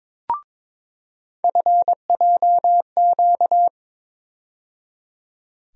Morse code, heard as a keyed tone, about 22 wpm, 700 Hz. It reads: FJQ
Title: USA Call Sign Postfixes - 3 Letters 22wpm